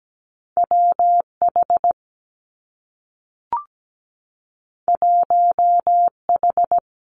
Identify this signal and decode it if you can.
Morse code, heard as a keyed tone, about 17 wpm, 700 Hz. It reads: WH 1H
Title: Two Challenging Letter-Number Combinations 17wpm